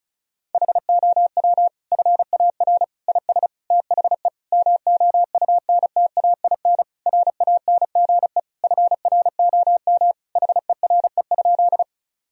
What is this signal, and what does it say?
Morse code, heard as a keyed tone, about 34 wpm, 700 Hz. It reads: HOW FAR IS THE MOUNTAIN RANGE FROM HERE?
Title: Sentences from Top 1000 Words - Review Part 4 - 35wpm